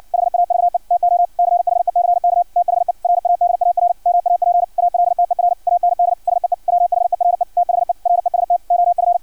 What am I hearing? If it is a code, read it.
Morse code, heard as a keyed tone, about 56 wpm, 700 Hz, with radio noise. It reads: HAVE TO CHECK THE RADAR MAP ALTER AND SEE WHERE THE REST OF
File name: archived / CRSnet